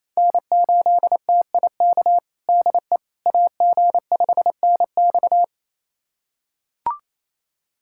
Morse code, heard as a keyed tone, about 28 wpm, 700 Hz. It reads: N8TIK DE AG5NX
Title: USA Callsign DE Callsign - 5 Characters - 28wpm